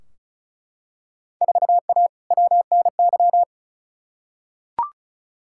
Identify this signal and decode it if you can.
Morse code, heard as a keyed tone, about 34 wpm, 700 Hz. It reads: 4A WNY
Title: Field Day Exchanges Encore - 35wpm